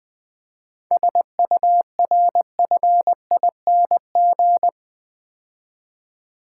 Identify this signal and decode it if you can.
Morse code, heard as a keyed tone, about 24 wpm, 700 Hz. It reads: SURFING
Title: Easy Words - Speed-Racing with Hints - 20wpm